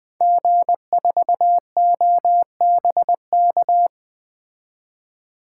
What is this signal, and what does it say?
Morse code, heard as a keyed tone, about 20 wpm, 700 Hz. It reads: G4OBK
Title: 2023 SOTA Activator Call Signs - 20wpm